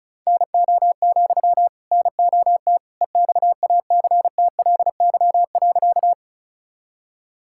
Morse code, heard as a keyed tone, about 36 wpm, 700 Hz. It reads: NO, NOT EXACTLY.
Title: Sentences from Top 1000 Words - Review Part 6 - 35wpm